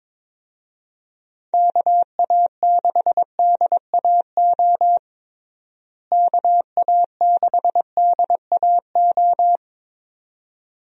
Morse code, heard as a keyed tone, about 21 wpm, 700 Hz. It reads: KA6DAO KA6DAO
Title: USA Call Signs - 6 Characters - Repeated 2x - 22wpm